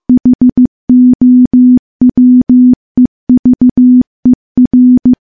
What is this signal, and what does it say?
Morse code, heard as a keyed tone, about 15 wpm, 250 Hz. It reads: HOWEVER
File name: morse / morse-training-15WPM